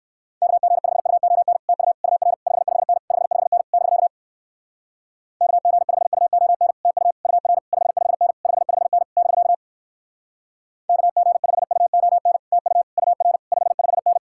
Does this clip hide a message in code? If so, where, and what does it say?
Yes: Morse code from 0.4 to 14.2 s, heard as a keyed tone, about 59 wpm, 700 Hz. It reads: KG5UON TU UR 55N 55N <BK> KG5UON TU UR 55N 55N <BK> KG5UON TU UR 55N